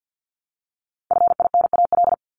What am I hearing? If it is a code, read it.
Morse code, heard as a keyed tone, about 50 wpm, 700 Hz. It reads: FINAL
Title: Top 500 Words 50wpm